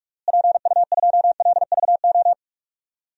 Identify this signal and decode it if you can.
Morse code, heard as a keyed tone, about 46 wpm, 700 Hz. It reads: PU2LVO
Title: DXCC-100 Call Signs - 6 Characters - Repeated 3x - 45wpm